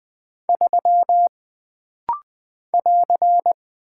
Morse code, heard as a keyed tone, about 20 wpm, 700 Hz. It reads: 3 <AR>
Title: Single Character (Letter:Number:Punctuation:Prosign) - Rapid-Fire - 20wpm